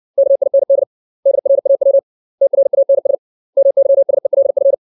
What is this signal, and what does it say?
Morse code, heard as a keyed tone, about 60 wpm, 550 Hz. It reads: KIND DRAW TRANS MYSELF
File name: 632 words / 632 Words 60WPM